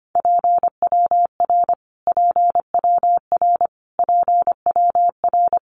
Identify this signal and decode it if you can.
Morse code, heard as a keyed tone, about 25 wpm, 700 Hz. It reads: PWR PWR PWR
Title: Abbreviations Repeated 3x - 25wpm